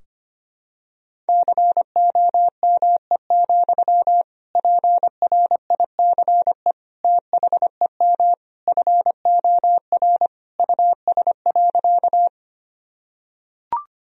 Morse code, heard as a keyed tone, about 25 wpm, 700 Hz. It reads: COME, PRICE THEM FOR US.
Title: Sentences from Top 1000 Words - Review Part 4 - 25wpm